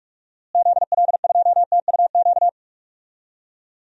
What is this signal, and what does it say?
Morse code, heard as a keyed tone, about 42 wpm, 700 Hz. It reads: ZL2TUQ
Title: DXCC-100 Call Signs - 6 Characters - Repeated 2x - 45wpm